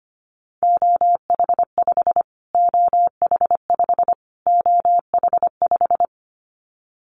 Morse code, heard as a keyed tone, about 25 wpm, 700 Hz. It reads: OH5 OH5 OH5
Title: Callsign Prefixes for Top 25 DXCC - Repeated 3x - 25wpm